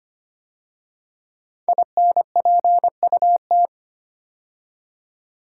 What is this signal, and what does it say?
Morse code, heard as a keyed tone, about 25 wpm, 700 Hz. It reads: INPUT
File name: 5 Letter Words - First 3 Letters Given - 25wpm